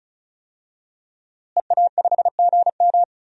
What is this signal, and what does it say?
Morse code, heard as a keyed tone, about 36 wpm, 700 Hz. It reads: EA5GM
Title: DXCC-100 Call Signs - 5 Characters - Repeated 2x - 35wpm